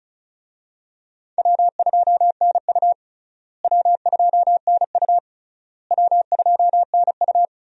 Morse code, heard as a keyed tone, about 36 wpm, 700 Hz. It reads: W2NU W2NU W2NU
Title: USA Call Signs - 4 Characters - Repeated 3x - 35wpm